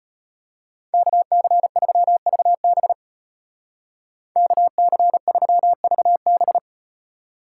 Morse code, heard as a keyed tone, about 38 wpm, 700 Hz. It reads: KC3VB KC3VB